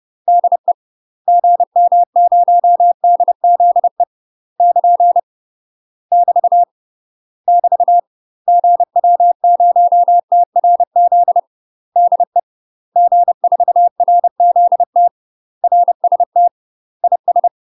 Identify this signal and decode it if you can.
Morse code, heard as a keyed tone, about 30 wpm, 700 Hz. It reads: DE GM0DZE <KN> <BT> <BT> GW0TRZ DE G4RZT RST IS
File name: UK MASTER QSOs TRAINING / 30WPM / UK MASTER QSOs 0005